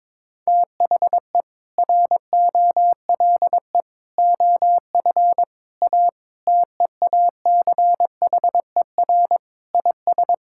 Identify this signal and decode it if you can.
Morse code, heard as a keyed tone, about 22 wpm, 700 Hz. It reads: THE ROLE OF A TEACHER IS
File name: practice / 22wpm / Sentences with 3-Word Phrases - 22wpm